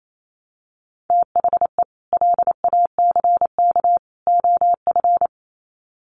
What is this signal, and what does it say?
Morse code, heard as a keyed tone, about 28 wpm, 700 Hz. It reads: THE LACK OF
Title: Sets of 3 Words 28wpm